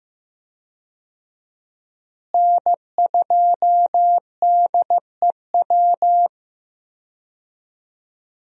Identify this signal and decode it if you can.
Morse code, heard as a keyed tone, about 15 wpm, 700 Hz. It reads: N2DEW
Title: USA Call Signs - 5 Characters - Repeated 3x - 15wpm